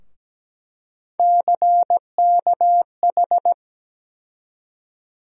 Morse code, heard as a keyed tone, about 17 wpm, 700 Hz. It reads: CKH
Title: USA Call Sign Postfixes - 3 Letters 17wpm